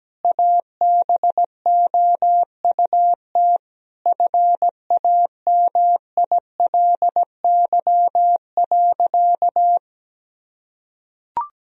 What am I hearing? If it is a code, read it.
Morse code, heard as a keyed tone, about 17 wpm, 700 Hz. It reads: ABOUT FAMILY.
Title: Sentences from Top 1000 Words - Review Part 3 - 17wpm